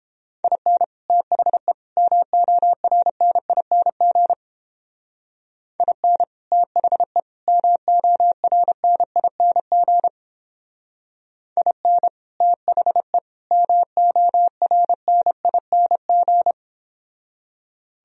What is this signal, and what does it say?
Morse code, heard as a keyed tone, about 29 wpm, 700 Hz. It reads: IN THE MORNING IN THE MORNING IN THE MORNING